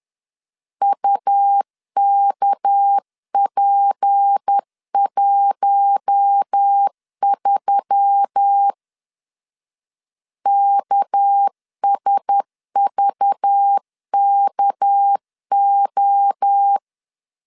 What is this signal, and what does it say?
Morse code, heard as a keyed tone, about 10 wpm, 800 Hz. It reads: UKP13 KSVKO
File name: LEZIONI / 6 PKX0 / 6 PK 10wpm 2436